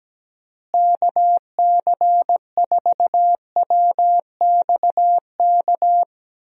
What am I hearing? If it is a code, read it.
Morse code, heard as a keyed tone, about 17 wpm, 700 Hz. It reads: KC4WXK
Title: USA Call Signs - 6 Characters - Repeated 3x - 17wpm